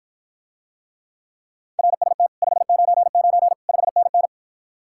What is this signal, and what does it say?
Morse code, heard as a keyed tone, about 53 wpm, 700 Hz. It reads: RST 599 5NN